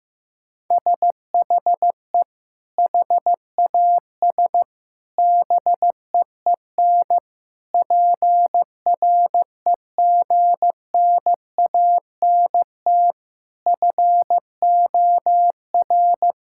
Morse code, heard as a keyed tone, about 15 wpm, 700 Hz. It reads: SHE HAS BEEN PREGNANT FOR
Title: Sentences from Top 1000 Words - Review Part 6 - 15wpm